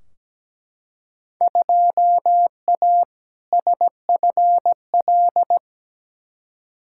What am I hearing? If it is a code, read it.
Morse code, heard as a keyed tone, about 17 wpm, 700 Hz. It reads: 2A SFL
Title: Field Day Exchanges Encore - 17wpm